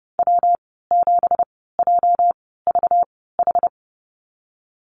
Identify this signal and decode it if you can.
Morse code, heard as a keyed tone, about 30 wpm, spaced as for 17 wpm, 700 Hz. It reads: W7JVH
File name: Lesson 37g - Letter x - Call Signs 30wpm